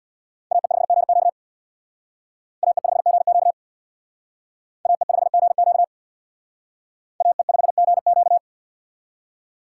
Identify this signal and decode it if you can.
Morse code, heard as a keyed tone, about 58 wpm, 700 Hz. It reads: AE5GQ AE5GQ AE5GQ AE5GQ
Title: Speed Racing - USA 2x2 Call Signs 50wpm